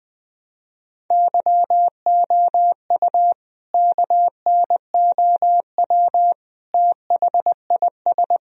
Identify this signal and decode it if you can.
Morse code, heard as a keyed tone, about 20 wpm, 700 Hz. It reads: YOU KNOW THIS
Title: Sentences from Top 1000 Words - Review Part 2 - 20wpm